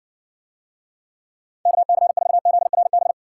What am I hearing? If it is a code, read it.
Morse code, heard as a keyed tone, about 60 wpm, 700 Hz. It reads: KC4ZRB